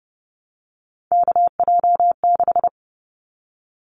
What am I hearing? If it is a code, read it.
Morse code, heard as a keyed tone, about 30 wpm, 700 Hz. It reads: KJ6
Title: Callsign Prefixes for Top 25 DXCC - Repeated 3x - 30wpm